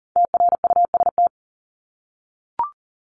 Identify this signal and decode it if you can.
Morse code, heard as a keyed tone, about 40 wpm, 700 Hz. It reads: TRUST
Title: Top 1751-2000 Words 40wpm